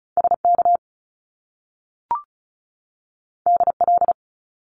Morse code, heard as a keyed tone, about 36 wpm, 700 Hz. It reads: SK DL
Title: Top Two Letter Combinations in Words 35wpm